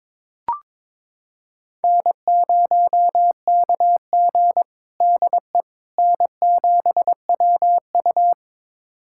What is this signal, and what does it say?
Morse code, heard as a keyed tone, about 22 wpm, 700 Hz. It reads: N0KG DE N7WU
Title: USA Callsign DE Callsign - 4 Characters - 22wpm